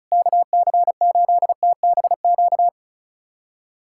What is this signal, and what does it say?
Morse code, heard as a keyed tone, about 36 wpm, 700 Hz. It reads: KC8TBQ
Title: USA Callsign DE Callsign 35wpm